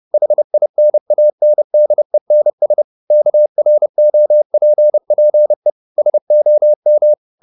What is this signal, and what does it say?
Morse code, heard as a keyed tone, about 30 wpm, 600 Hz. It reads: HINANDENS KROPPE SOM
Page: Ferie 30 wpm